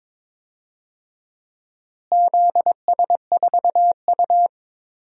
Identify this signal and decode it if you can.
Morse code, heard as a keyed tone, about 22 wpm, 700 Hz. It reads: ZS4U
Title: DXCC-100 Call Signs - 4 Characters - Repeated 2x - 22wpm